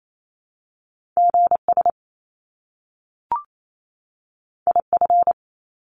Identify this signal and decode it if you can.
Morse code, heard as a keyed tone, about 28 wpm, 700 Hz. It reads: GS IF
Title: Top Two Letter Combinations in Words 28wpm